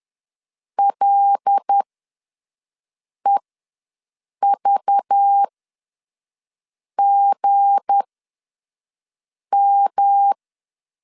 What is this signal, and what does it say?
Morse code, heard as a keyed tone, about 10 wpm, spaced as for 5 wpm, 800 Hz. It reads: LEVGM
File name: LEZIONI / 8 YJ45 / 8 YJ4 ws 10wpm 4020